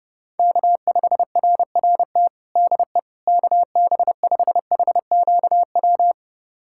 Morse code, heard as a keyed tone, about 30 wpm, 690 Hz. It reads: K5RRT DE KB5HQW
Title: USA Callsign DE Callsign 30wpm